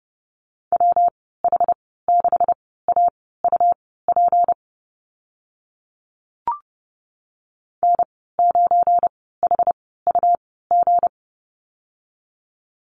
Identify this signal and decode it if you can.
Morse code, heard as a keyed tone, about 30 wpm, spaced as for 17 wpm, 700 Hz. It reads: WH6AUP N9HUG
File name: Lesson 37g - Letter x - Call Signs 30wpm